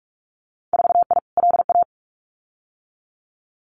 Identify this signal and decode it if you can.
Morse code, heard as a keyed tone, about 46 wpm, 700 Hz. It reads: 4I LA
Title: Winter Field Day Exchanges Encore 45wpm